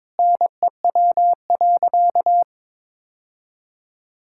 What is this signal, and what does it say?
Morse code, heard as a keyed tone, about 22 wpm, 700 Hz. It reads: NEW.
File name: practice / 22wpm / Sentences with 8-Word Phrases - 22wpm